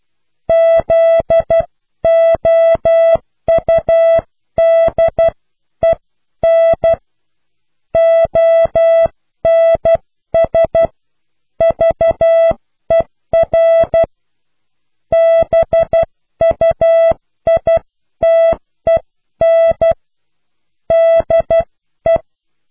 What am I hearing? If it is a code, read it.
Morse code, heard as a keyed tone, about 12 wpm, 660 Hz. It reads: ZOUDEN ONS VER BUITEN DE